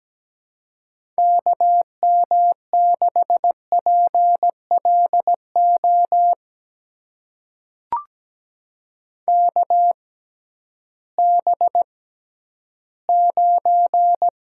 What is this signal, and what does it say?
Morse code, heard as a keyed tone, about 17 wpm, spaced as for 11 wpm, 700 Hz. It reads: KM6PLO KB9
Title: USA Call Signs - 6 Characters - 6x Character Spacing - 17wpm